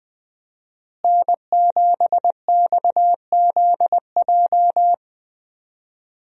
Morse code, heard as a keyed tone, about 20 wpm, 700 Hz. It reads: N7XZJ
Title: USA Call Signs - 5 Characters - Repeated 3x - 20wpm